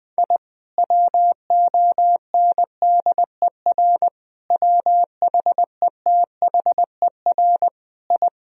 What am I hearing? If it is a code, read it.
Morse code, heard as a keyed tone, about 20 wpm, 700 Hz. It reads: I WONDER WHETHER I
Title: Sentences from Top 1000 Words - Review Part 3 - 20wpm